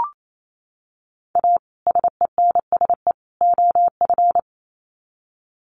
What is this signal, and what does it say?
Morse code, heard as a keyed tone, about 28 wpm, 700 Hz. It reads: A SENSE OF
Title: Sets of 3 Words 28wpm